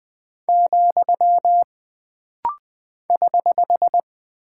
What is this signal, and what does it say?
Morse code, heard as a keyed tone, about 20 wpm, 700 Hz. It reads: , <HH>
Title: Single Punctuation-Prosign - Rapid-Fire - 20wpm